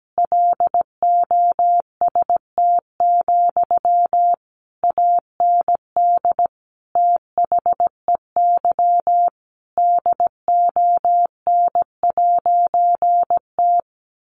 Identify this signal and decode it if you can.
Morse code, heard as a keyed tone, about 17 wpm, 700 Hz. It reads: LOST, AND THEY DON'T
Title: A Sound Of Thunder 17wpm